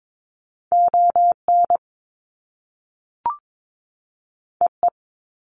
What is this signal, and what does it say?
Morse code, heard as a keyed tone, about 22 wpm, 700 Hz. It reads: ON EE
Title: Top Two Letter Combinations in Words 22wpm